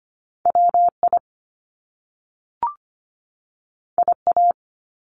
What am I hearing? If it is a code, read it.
Morse code, heard as a keyed tone, about 25 wpm, 700 Hz. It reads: WI IA
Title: US States 25wpm